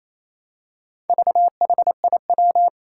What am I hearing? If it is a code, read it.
Morse code, heard as a keyed tone, about 28 wpm, 700 Hz. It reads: VHIW
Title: Four Challenging Letter-Number Combinations 28wpm